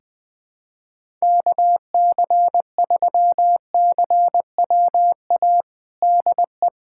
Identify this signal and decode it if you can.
Morse code, heard as a keyed tone, about 20 wpm, 700 Hz. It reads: KC3CWA DE
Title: USA Callsign DE Callsign 20wpm